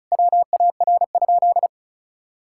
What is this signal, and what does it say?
Morse code, heard as a keyed tone, about 34 wpm, 700 Hz. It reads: WAR?
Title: Sentences from Top 1000 Words - Review Part 6 - 35wpm